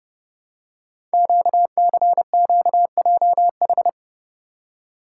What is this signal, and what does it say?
Morse code, heard as a keyed tone, about 28 wpm, 700 Hz. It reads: QCQJH
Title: Code Groups - 5 Letters - 30wpm